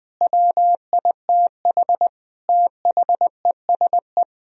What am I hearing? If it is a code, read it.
Morse code, heard as a keyed tone, about 20 wpm, 700 Hz. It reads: WITH THESE
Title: Sentences from Top 1000 Words - Review Part 1 - 20wpm